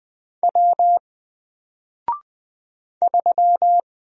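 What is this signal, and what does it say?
Morse code, heard as a keyed tone, about 20 wpm, 700 Hz. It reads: W 3
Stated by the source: Single letter and number Morse drill at 20 wpm, from N4LCD.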